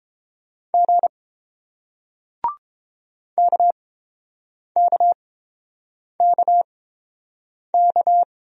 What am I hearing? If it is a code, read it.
Morse code, heard as a keyed tone, about 31 wpm, spaced as for 7 wpm, 700 Hz. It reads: G KKKK